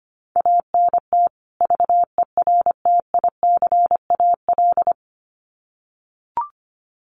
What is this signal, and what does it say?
Morse code, heard as a keyed tone, about 25 wpm, 700 Hz. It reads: ANT VERTICAL
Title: Common QSO Phrases 25wpm